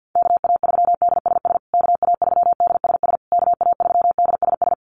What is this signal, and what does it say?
Morse code, heard as a keyed tone, about 48 wpm, 700 Hz. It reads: KA3DSS KA3DSS KA3DSS
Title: USA Call Signs Repeated 3x 50wpm